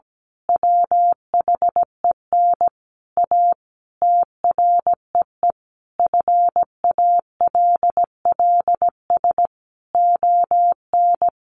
Morse code, heard as a keyed tone, about 17 wpm, 700 Hz. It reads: WHEN A TREE FALLS ON
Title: A Sound Of Thunder 17wpm